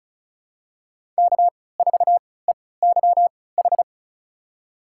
Morse code, heard as a keyed tone, about 36 wpm, spaced as for 20 wpm, 700 Hz. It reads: K4EYH
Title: USA Call Signs - 5 Characters - 3x Character Spacing - 35wpm